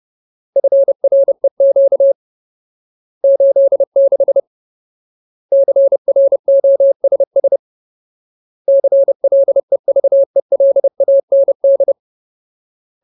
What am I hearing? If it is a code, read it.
Morse code, heard as a keyed tone, about 30 wpm, 550 Hz. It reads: FREQ 86 CROSS CLEVELAND
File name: Ultimate Word List / 30WPM Ultimate